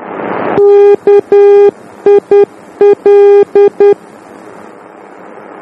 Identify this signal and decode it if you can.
Morse code, heard as a keyed tone, about 10 wpm, 400 Hz. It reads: KIL
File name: germany / mx / KIL-D